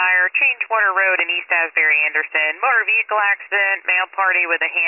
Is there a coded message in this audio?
Yes: Emergency Alert System (SAME) tones from 0.7 to 3.7 s, heard as harsh buzzing bursts, with speech.